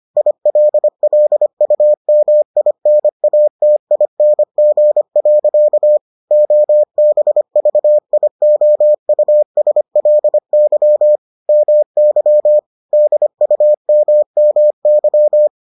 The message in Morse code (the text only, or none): ILLUMINATING. OBVIOUSLY MY DUMMY